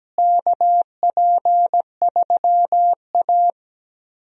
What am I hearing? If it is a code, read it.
Morse code, heard as a keyed tone, about 17 wpm, 700 Hz. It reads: KP3A
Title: USA Call Signs - 4 Characters - Repeated 2x - 17wpm